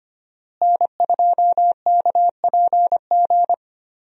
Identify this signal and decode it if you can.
Morse code, heard as a keyed tone, about 25 wpm, 700 Hz. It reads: N2KPG
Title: USA Call Signs - 5 Characters - Repeated 2x - 25wpm